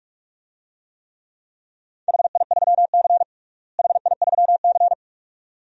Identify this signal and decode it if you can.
Morse code, heard as a keyed tone, about 43 wpm, 700 Hz. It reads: HI3C HI3C
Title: DXCC-100 Call Signs - 4 Characters - Repeated 2x - 45wpm